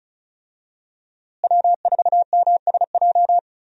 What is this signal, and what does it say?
Morse code, heard as a keyed tone, about 36 wpm, 700 Hz. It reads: W4MSJ
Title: USA Call Signs - 5 Characters - Repeated 2x - 35wpm